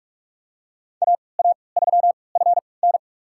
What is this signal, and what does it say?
Morse code, heard as a keyed tone, about 46 wpm, spaced as for 26 wpm, 700 Hz. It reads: AA3FN
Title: USA Call Signs - 5 Characters - 3x Character Spacing - 45wpm